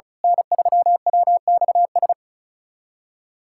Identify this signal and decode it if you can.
Morse code, heard as a keyed tone, about 34 wpm, 700 Hz. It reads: N3WXS
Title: USA Callsign DE Callsign - 5 Characters - 35wpm